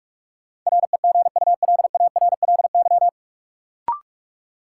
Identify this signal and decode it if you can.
Morse code, heard as a keyed tone, about 46 wpm, 700 Hz. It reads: REGULARLY
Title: Sentences from Top 1400 Words - New Word Drill - 45wpm